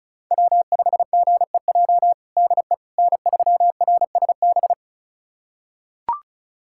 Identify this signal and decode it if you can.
Morse code, heard as a keyed tone, about 36 wpm, 700 Hz. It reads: W5GEJ DE N3RSB
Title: USA Callsign DE Callsign - 5 Characters - 35wpm